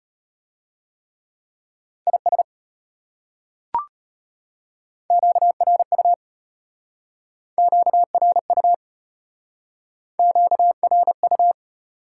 Morse code, heard as a keyed tone, about 33 wpm, 700 Hz. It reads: IS QRU QRU QRU